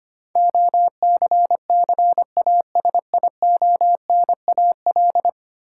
Morse code, heard as a keyed tone, about 25 wpm, 700 Hz. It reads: OCCASIONAL
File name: Ten Letter Words - 25wpm